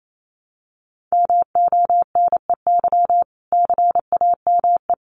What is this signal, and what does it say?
Morse code, heard as a keyed tone, about 28 wpm, 700 Hz. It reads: MONEY CAME
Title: Sentences from Top 1100 Words 28wpm